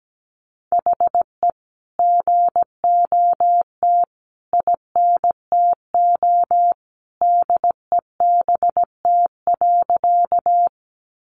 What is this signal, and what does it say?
Morse code, heard as a keyed tone, about 17 wpm, 700 Hz. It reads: HE GOT INTO DEBT.
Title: Sentences from Top 1300 Words 17wpm